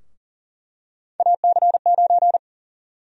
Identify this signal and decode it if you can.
Morse code, heard as a keyed tone, about 40 wpm, 700 Hz. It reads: AC9
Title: USA Call Sign Prefixes - 40wpm